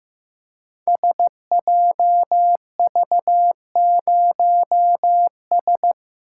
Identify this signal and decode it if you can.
Morse code, heard as a keyed tone, about 15 wpm, 700 Hz. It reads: SJV0S
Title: Code Groups - 5 Letter:Numbers - 15wpm